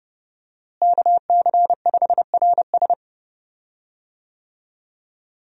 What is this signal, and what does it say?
Morse code, heard as a keyed tone, about 31 wpm, 700 Hz. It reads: KC5RS
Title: USA Call Signs - 5 Characters - Repeated 2x - 30wpm